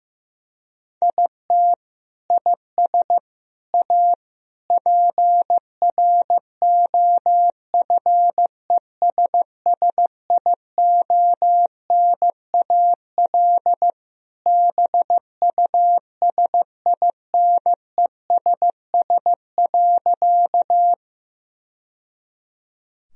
Morse code, heard as a keyed tone, about 15 wpm, 700 Hz. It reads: IT IS A PROFESSIONAL BUSINESS.
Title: Sentences from Top 1000 Words - Review Part 4 - 15wpm